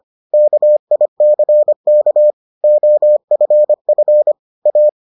Morse code, heard as a keyed tone, about 25 wpm, 600 Hz. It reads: KICK OFF A